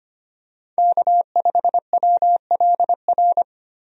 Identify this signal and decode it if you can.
Morse code, heard as a keyed tone, about 25 wpm, 700 Hz. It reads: K5WLR
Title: USA Call Signs - 5 Characters - 25wpm